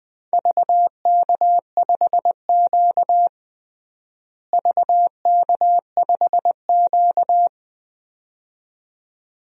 Morse code, heard as a keyed tone, about 21 wpm, 700 Hz. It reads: VK5Q VK5Q
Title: DXCC-100 Call Signs - 4 Characters - Repeated 2x - 20wpm